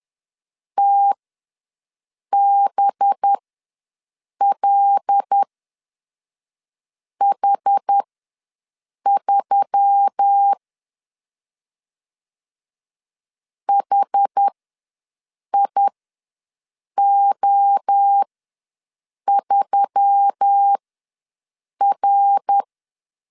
Morse code, heard as a keyed tone, about 10 wpm, spaced as for 5 wpm, 800 Hz. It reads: TBLH3 HIO3R
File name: LEZIONI / 7 GF9Z / 7 GF9 ws 10wpm 4034